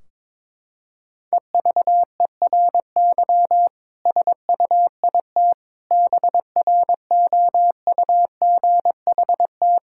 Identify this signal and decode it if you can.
Morse code, heard as a keyed tone, about 22 wpm, 700 Hz. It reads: EVERY SUIT BROUGHT
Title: Sentences from Top 1600 Words 22wpm